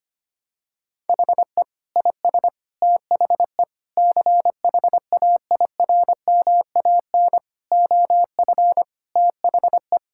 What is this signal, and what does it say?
Morse code, heard as a keyed tone, about 25 wpm, 700 Hz. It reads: HE IS THE CHAIRMAN OF THE
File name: practice / 25wpm / Sentences with 3-Word Phrases - 25wpm